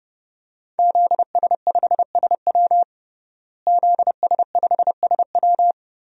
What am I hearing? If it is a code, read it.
Morse code, heard as a keyed tone, about 30 wpm, 700 Hz. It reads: ZS5SW ZS5SW
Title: DXCC-100 Call Signs - 5 Characters - Repeated 2x - 30wpm